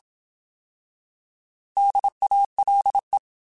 Morse code, heard as a keyed tone, about 27 wpm, 800 Hz. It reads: DALE
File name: Alphabetical / Alphabetical mp3s / Ds @25